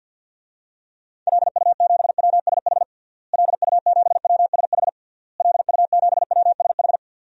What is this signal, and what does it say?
Morse code, heard as a keyed tone, about 52 wpm, 700 Hz. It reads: LU7WSH LU7WSH LU7WSH
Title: DXCC-100 Call Signs - 6 Characters - Repeated 3x - 50wpm